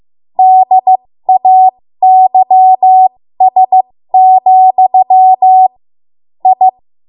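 Morse code, heard as a keyed tone, about 15 wpm, spaced as for 13 wpm, 750 Hz. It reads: DAYS, I
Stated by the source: 13 wpm Morse code practice archive transmission.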